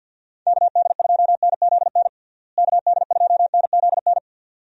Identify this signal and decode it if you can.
Morse code, heard as a keyed tone, about 46 wpm, 700 Hz. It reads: KD2NZN KD2NZN
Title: USA Call Signs - 6 Characters - Repeated 2x - 50wpm